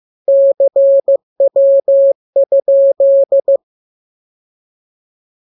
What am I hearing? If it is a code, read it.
Morse code, heard as a keyed tone, about 15 wpm, 550 Hz. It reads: CW?
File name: jokes / joke 26 15 0000